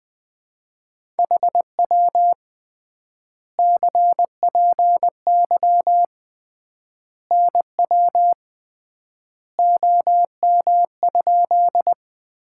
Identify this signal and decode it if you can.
Morse code, heard as a keyed tone, about 20 wpm, 700 Hz. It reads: HW CPY NW OM?
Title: Common QSO Phrases - 3x Word-Spacing - 20wpm